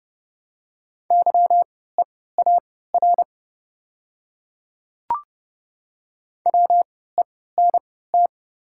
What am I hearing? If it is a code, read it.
Morse code, heard as a keyed tone, about 30 wpm, spaced as for 17 wpm, 690 Hz. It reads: YEAR WENT
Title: 3x Character Spacing - Top 300 Words - 30wpm